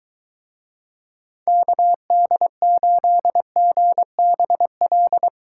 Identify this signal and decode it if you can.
Morse code, heard as a keyed tone, about 23 wpm, 700 Hz. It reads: KD8GBL